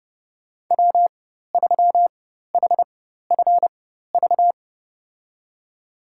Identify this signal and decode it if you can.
Morse code, heard as a keyed tone, about 29 wpm, spaced as for 14 wpm, 700 Hz. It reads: W3HFV
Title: USA Call Signs - 5 Characters - 4x Character Spacing - 30wpm